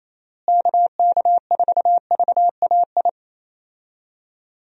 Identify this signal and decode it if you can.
Morse code, heard as a keyed tone, about 27 wpm, 700 Hz. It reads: KK4VAI
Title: USA Call Signs - 6 Characters - Repeated 2x - 28wpm